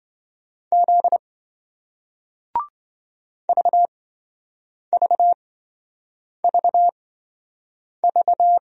Morse code, heard as a keyed tone, about 27 wpm, spaced as for 7 wpm, 700 Hz. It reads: Z VVVV